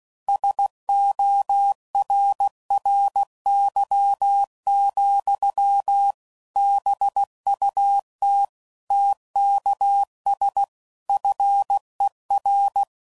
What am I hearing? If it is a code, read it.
Morse code, heard as a keyed tone, about 16 wpm, 800 Hz. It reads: SORRY, BUT TKS FER
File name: CW Code Groups @15WPM / QSOs / kc7sac